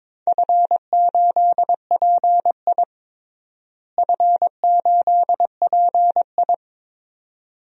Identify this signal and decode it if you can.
Morse code, heard as a keyed tone, about 22 wpm, 700 Hz. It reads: F8PI F8PI
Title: DXCC-100 Call Signs - 4 Characters - Repeated 2x - 22wpm